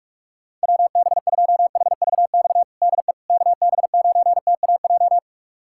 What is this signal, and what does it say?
Morse code, heard as a keyed tone, about 42 wpm, 700 Hz. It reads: WB2HVX DE KB9TAJ
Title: USA Callsign DE Callsign - 6 Characters - 45wpm